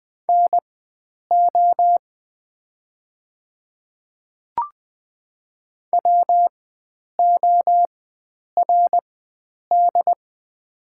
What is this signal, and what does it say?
Morse code, heard as a keyed tone, about 20 wpm, spaced as for 9 wpm, 690 Hz. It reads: NO WORD
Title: 4x Character Spacing - Top 100 Words - 20wpm